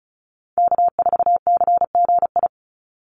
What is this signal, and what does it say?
Morse code, heard as a keyed tone, about 34 wpm, 700 Hz. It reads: K4CGI
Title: USA Call Signs 35wpm